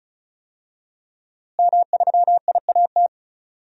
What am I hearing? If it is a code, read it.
Morse code, heard as a keyed tone, about 36 wpm, 700 Hz. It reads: M3IAT
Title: DXCC-100 Call Signs - 5 Characters - Repeated 2x - 35wpm